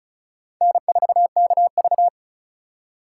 Morse code, heard as a keyed tone, about 36 wpm, 700 Hz. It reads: N4KV
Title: USA Call Signs - 4x Character Spacing - 35wpm